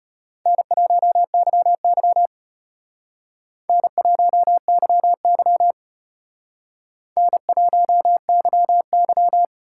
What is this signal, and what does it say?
Morse code, heard as a keyed tone, about 32 wpm, 700 Hz. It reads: N1YY N1YY N1YY